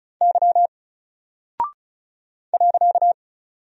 Morse code, heard as a keyed tone, about 36 wpm, 700 Hz. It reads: Y .
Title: Single Character (Letter:Number:Punctuation:Prosign) 35wpm